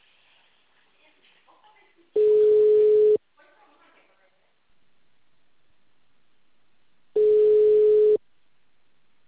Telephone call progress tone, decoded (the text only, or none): ringback tone